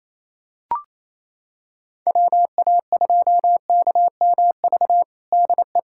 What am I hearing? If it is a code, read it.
Morse code, heard as a keyed tone, about 28 wpm, 700 Hz. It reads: WA2KMV DE
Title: USA Callsign DE Callsign 28wpm